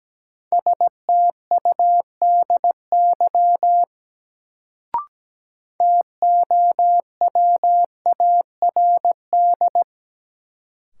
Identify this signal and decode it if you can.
Morse code, heard as a keyed tone, about 17 wpm, 700 Hz. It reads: STUDY TOWARD
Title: Sets of 5 Words - New Word Drill - 17wpm